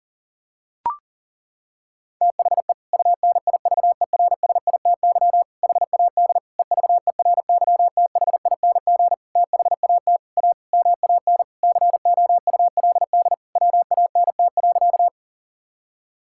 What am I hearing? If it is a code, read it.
Morse code, heard as a keyed tone, about 37 wpm, 700 Hz. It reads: THE UNIVERSITY HAD EVERYTHING THAT A MAN COULD WANT.
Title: Sentences from Top 1000 Words - Review Part 5 - 40wpm